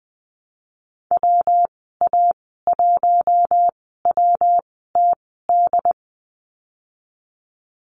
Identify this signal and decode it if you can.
Morse code, heard as a keyed tone, about 20 wpm, spaced as for 15 wpm, 700 Hz. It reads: WA1WTD
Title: 147 Lesson 24g - Letter y - Call Signs 20wpm